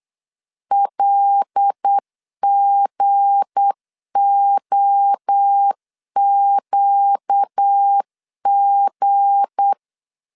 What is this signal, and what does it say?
Morse code, heard as a keyed tone, about 8 wpm, 800 Hz. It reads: LGOQG
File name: LEZIONI / 7 GF9Z / 7 G 8wpm 3118